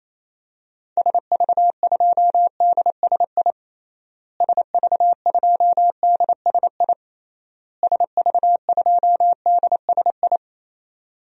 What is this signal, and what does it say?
Morse code, heard as a keyed tone, about 28 wpm, 700 Hz. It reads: SV2DSI SV2DSI SV2DSI
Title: DXCC-100 Call Signs - 6 Characters - Repeated 3x - 28wpm